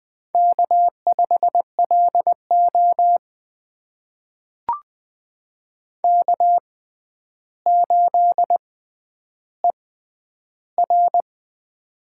Morse code, heard as a keyed tone, about 20 wpm, spaced as for 10 wpm, 700 Hz. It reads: K5LO K8ER
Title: USA Call Signs - 4 Characters - 6x Character Spacing - 20wpm